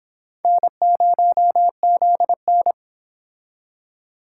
Morse code, heard as a keyed tone, about 31 wpm, 700 Hz. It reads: N0ZN